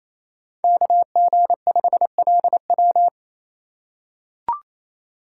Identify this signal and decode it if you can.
Morse code, heard as a keyed tone, about 28 wpm, 700 Hz. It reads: KG5LW
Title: USA Call Signs - 5 Characters - 2x Character Spacing - 28wpm